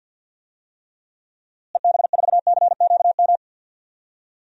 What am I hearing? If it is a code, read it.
Morse code, heard as a keyed tone, about 48 wpm, 700 Hz. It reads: EB4CQM
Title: DXCC-100 Call Signs - 6 Characters - Repeated 3x - 50wpm